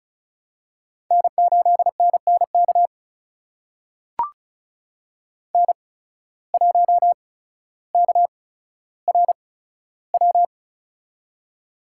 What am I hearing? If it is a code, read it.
Morse code, heard as a keyed tone, about 34 wpm, spaced as for 15 wpm, 700 Hz. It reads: N8NNK N1KRW
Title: USA Call Signs - 5 Characters - 8x Character Spacing - 35wpm